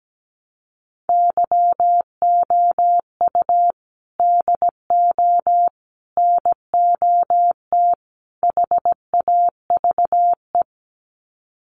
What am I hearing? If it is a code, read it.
Morse code, heard as a keyed tone, about 17 wpm, 700 Hz. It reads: YOU DO NOT HAVE
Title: Sets of 4 Words 17wpm